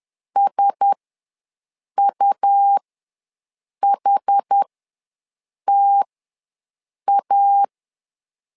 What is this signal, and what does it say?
Morse code, heard as a keyed tone, about 10 wpm, spaced as for 6 wpm, 800 Hz. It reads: SUHTA
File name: LEZIONI / 4 Q1S3 ws 10wpm 2437